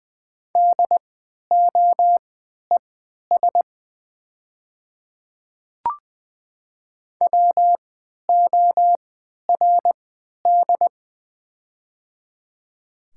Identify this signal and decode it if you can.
Morse code, heard as a keyed tone, about 20 wpm, spaced as for 11 wpm, 700 Hz. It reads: DOES WORD
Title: 3x Character Spacing - Top 300 Words - 20wpm